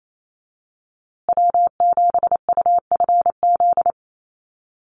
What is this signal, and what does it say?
Morse code, heard as a keyed tone, about 28 wpm, 700 Hz. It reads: W7UFZ
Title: USA Call Signs Repeated 3x 28wpm